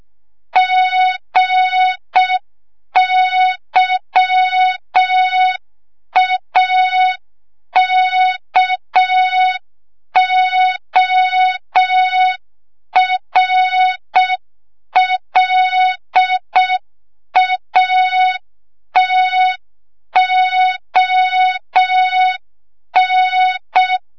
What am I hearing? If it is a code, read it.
Morse code, heard as a keyed tone, about 6 wpm, 3700 Hz. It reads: GYAKORLATON
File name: Morze-HA2MN / hang / 25 GYAKORLAT